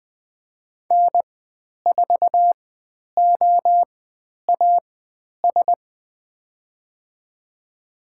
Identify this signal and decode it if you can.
Morse code, heard as a keyed tone, about 20 wpm, spaced as for 10 wpm, 700 Hz. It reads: N4OAS
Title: 095 Lesson 16g - Letter c - Call Signs 20wpm